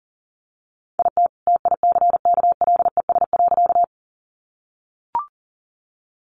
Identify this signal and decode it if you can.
Morse code, heard as a keyed tone, about 40 wpm, 700 Hz. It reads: IT TICKLES.
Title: Additional Easy Phrases 40wpm